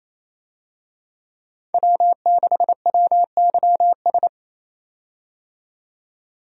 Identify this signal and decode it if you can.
Morse code, heard as a keyed tone, about 28 wpm, 700 Hz. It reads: W6WYS
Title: USA Call Signs - 5 Characters - 28wpm